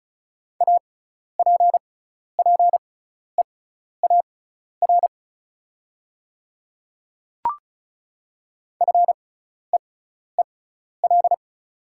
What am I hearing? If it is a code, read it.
Morse code, heard as a keyed tone, about 36 wpm, spaced as for 12 wpm, 690 Hz. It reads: APPEAR FEEL
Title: 6x Character Spacing - Top 500 Words - 35wpm